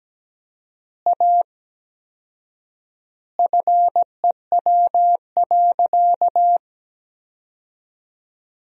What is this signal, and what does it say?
Morse code, heard as a keyed tone, about 17 wpm, 700 Hz. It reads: A FEW.
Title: 4x Word-Spacing - Sentences from Top 300 Words - 17wpm